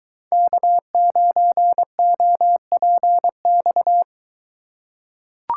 Morse code, heard as a keyed tone, about 22 wpm, 700 Hz. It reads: K9OPX